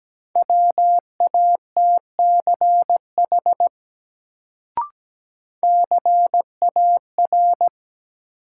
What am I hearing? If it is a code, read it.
Morse code, heard as a keyed tone, about 17 wpm, 700 Hz. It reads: WATCH CAR
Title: Sentences from Top 300 Words - New Word Drill - 17wpm